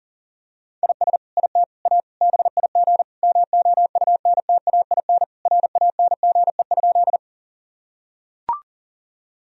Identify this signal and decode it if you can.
Morse code, heard as a keyed tone, about 40 wpm, 700 Hz. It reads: IS IT A BIG MOUNTAIN RANGE?
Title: Sentences from Top 1000 Words - Review Part 4 - 40wpm